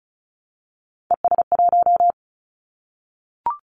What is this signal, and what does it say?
Morse code, heard as a keyed tone, about 36 wpm, 700 Hz. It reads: ES1
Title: Callsign Prefixes for Top 50 DXCC - 35wpm